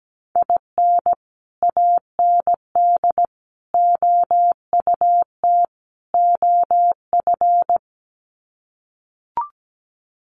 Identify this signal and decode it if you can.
Morse code, heard as a keyed tone, about 17 wpm, 700 Hz. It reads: IN AND OUT OF
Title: Sets of 4 Words 17wpm